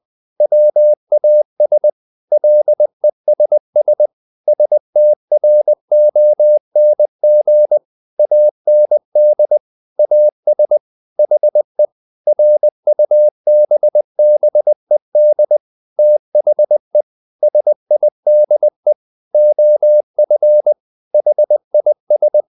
Morse code, heard as a keyed tone, about 20 wpm, 600 Hz. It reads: WAS LESS STRONG AND AS HE RUBBED THE SIDE OF HIS
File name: the old man and the sea part 2 20wpm